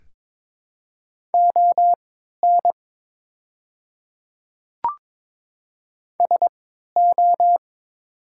Morse code, heard as a keyed tone, about 22 wpm, spaced as for 13 wpm, 700 Hz. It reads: ON SO
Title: 3x Character Spacing - Top 300 Words - 22wpm